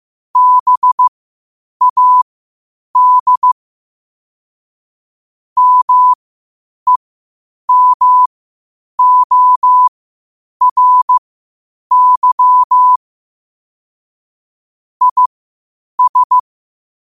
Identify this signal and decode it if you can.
Morse code, heard as a keyed tone, about 15 wpm, spaced as for 8 wpm, 1000 Hz. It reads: BAD MEMORY IS
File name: qotdfiles / 8wpm